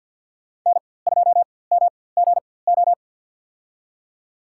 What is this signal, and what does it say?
Morse code, heard as a keyed tone, about 49 wpm, spaced as for 23 wpm, 700 Hz. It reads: N2MGO
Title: USA Call Signs - 5 Characters - 4x Character Spacing - 50wpm